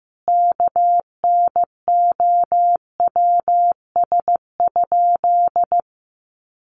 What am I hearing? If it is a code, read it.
Morse code, heard as a keyed tone, about 15 wpm, 700 Hz. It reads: KNOWS?
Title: A Sound Of Thunder 15wpm